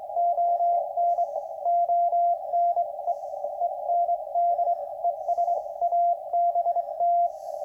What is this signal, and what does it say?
Morse code, heard as a keyed tone, about 23 wpm, 690 Hz, with radio noise. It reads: ON ONE END ES ABT